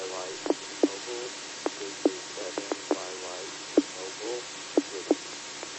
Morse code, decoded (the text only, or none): none